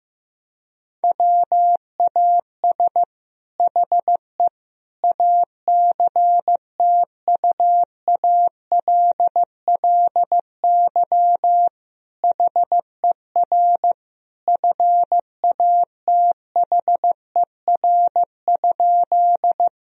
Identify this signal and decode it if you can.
Morse code, heard as a keyed tone, about 15 wpm, 700 Hz. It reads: WAS HE ACTUALLY HER FATHER?
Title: Sentences from Top 1000 Words - Review Part 4 - 15wpm